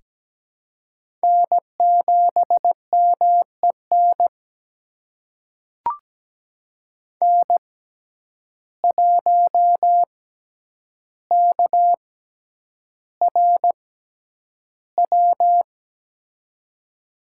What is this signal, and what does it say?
Morse code, heard as a keyed tone, about 17 wpm, spaced as for 9 wpm, 700 Hz. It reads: N7MEN N1KRW
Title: USA Call Signs - 5 Characters - 6x Character Spacing - 17wpm